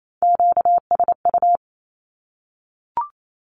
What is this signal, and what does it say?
Morse code, heard as a keyed tone, about 28 wpm, 700 Hz. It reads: QSU
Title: Q-Codes Repeated 3x 28wpm